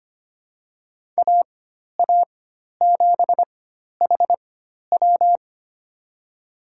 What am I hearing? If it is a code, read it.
Morse code, heard as a keyed tone, about 25 wpm, spaced as for 12 wpm, 700 Hz. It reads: AA7HW
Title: USA Call Signs - 5 Characters - 4x Character Spacing - 25wpm